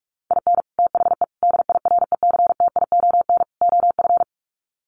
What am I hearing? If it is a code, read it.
Morse code, heard as a keyed tone, about 44 wpm, 700 Hz. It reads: IN THE DIRECTION OF
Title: Sets of 4 Words 45wpm